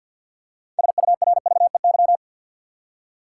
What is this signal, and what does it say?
Morse code, heard as a keyed tone, about 52 wpm, 700 Hz. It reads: SURVEY
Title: Six Letter Words - 50wpm